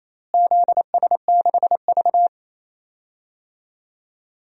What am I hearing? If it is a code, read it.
Morse code, heard as a keyed tone, about 28 wpm, 700 Hz. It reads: ZS6V
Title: DXCC-100 Call Signs - 4 Characters - Repeated 2x - 28wpm